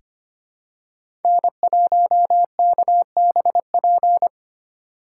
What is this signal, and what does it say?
Morse code, heard as a keyed tone, about 25 wpm, 700 Hz. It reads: N1KBP
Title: USA Call Signs - 5 Characters - 4x Character Spacing - 25wpm